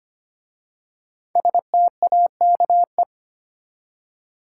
Morse code, heard as a keyed tone, about 25 wpm, 700 Hz. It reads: STAKE
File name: 5 Letter Words - First 3 Letters Given - 25wpm